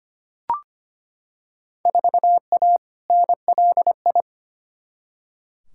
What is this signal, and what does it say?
Morse code, heard as a keyed tone, about 25 wpm, 700 Hz. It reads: 4A NLI
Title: Field Day Exchanges Encore - 25wpm